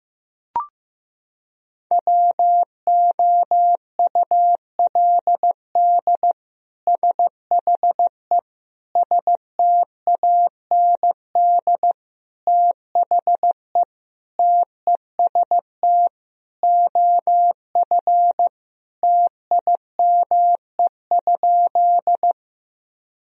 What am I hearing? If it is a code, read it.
Morse code, heard as a keyed tone, about 15 wpm, 700 Hz. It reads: WOULD SHE STAND THE TEST OF TIME?
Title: Sentences from Top 1000 Words - Review Part 3 - 15wpm